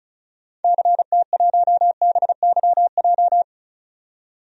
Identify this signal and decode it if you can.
Morse code, heard as a keyed tone, about 34 wpm, 700 Hz. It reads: CT1BYJ
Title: DXCC-100 Call Signs - 6 Characters - 35wpm